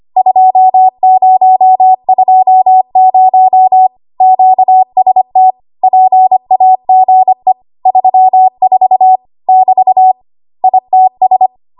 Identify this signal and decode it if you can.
Morse code, heard as a keyed tone, about 25 wpm, 750 Hz. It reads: 2020 QST PAGE 34 <BT> ITS
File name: Archive / 25 WPM / 25WPM